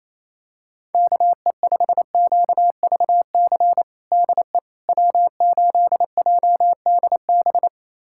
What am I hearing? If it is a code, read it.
Morse code, heard as a keyed tone, about 28 wpm, 700 Hz. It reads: KE5QVC DE W8JDB
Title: USA Callsign DE Callsign 28wpm